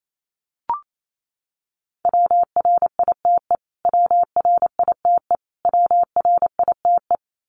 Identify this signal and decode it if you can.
Morse code, heard as a keyed tone, about 28 wpm, 700 Hz. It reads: WRITE WRITE WRITE
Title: Top 200 Words Repeated 28wpm